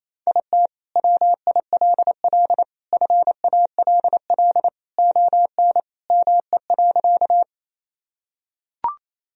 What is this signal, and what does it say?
Morse code, heard as a keyed tone, about 28 wpm, 700 Hz. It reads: IT WILL FALL ON ME.
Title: Sentences from Top 1000 Words - Review Part 2 - 28wpm